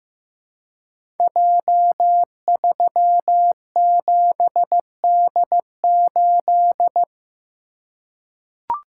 Morse code, heard as a keyed tone, about 15 wpm, 700 Hz. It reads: J37D8
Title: Code Groups - 5 Letter:Numbers - 15wpm